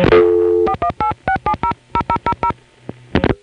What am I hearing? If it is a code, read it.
DTMF, heard as telephone touch tones, tones of about 80 ms. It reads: 4186700000